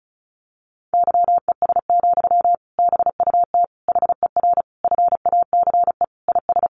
Morse code, heard as a keyed tone, about 34 wpm, 700 Hz. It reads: YES, BUT HER FACE IS
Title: Sentences from Top 1200 Words 35wpm